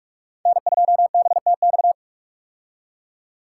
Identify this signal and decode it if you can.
Morse code, heard as a keyed tone, about 44 wpm, 700 Hz. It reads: N2BTX
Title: K1USN SST QSO Practice - 45wpm